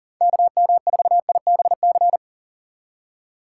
Morse code, heard as a keyed tone, about 38 wpm, 700 Hz. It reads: KM4IBC
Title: USA Call Signs - 2x Character Spacing - 40wpm